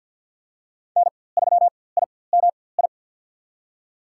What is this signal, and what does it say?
Morse code, heard as a keyed tone, about 51 wpm, spaced as for 23 wpm, 700 Hz. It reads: N3IMI
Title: USA Call Signs - 4x Character Spacing - 50wpm